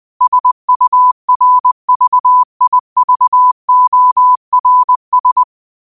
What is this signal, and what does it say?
Morse code, heard as a keyed tone, about 20 wpm, 1000 Hz. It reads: SURVIVORS
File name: qotdfiles / 20wpm